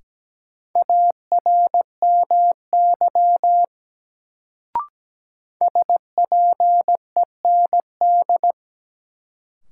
Morse code, heard as a keyed tone, about 17 wpm, 700 Hz. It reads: ARMY SPEND
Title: Sentences from Top 800 Words - New Word Drill - 17wpm